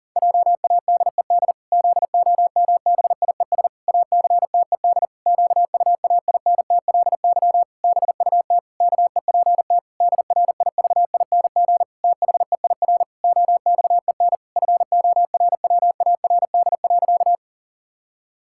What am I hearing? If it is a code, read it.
Morse code, heard as a keyed tone, about 40 wpm, 700 Hz. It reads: JADED ZOMBIES ACTED QUAINTLY BUT KEPT DRIVING THEIR OXEN FORWARD.
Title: Pangram Challenge v2 - 40wpm